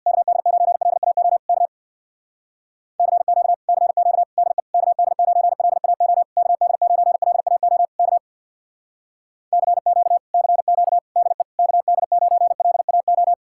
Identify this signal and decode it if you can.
Morse code, heard as a keyed tone, about 64 wpm, 700 Hz. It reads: KD9LAO K CQ CQ DE KD9LAO KD9LAO K CQ CQ DE KD9LAO